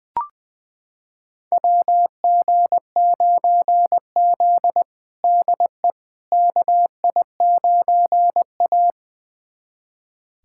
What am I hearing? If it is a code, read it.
Morse code, heard as a keyed tone, about 21 wpm, 700 Hz. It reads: WG9Z DE KI9A
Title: USA Callsign DE Callsign - 4 Characters - 20wpm